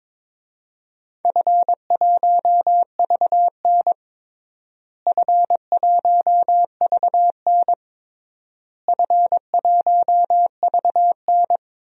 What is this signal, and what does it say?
Morse code, heard as a keyed tone, about 22 wpm, 700 Hz. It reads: F1VN F1VN F1VN
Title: DXCC-100 Call Signs - 4 Characters - Repeated 3x - 22wpm